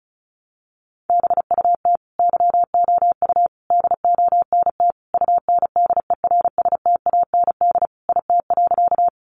Morse code, heard as a keyed tone, about 34 wpm, 700 Hz. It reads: BUT YOU DONT UNDERSTAND IT.
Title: Sentences from Top 600 Words 35wpm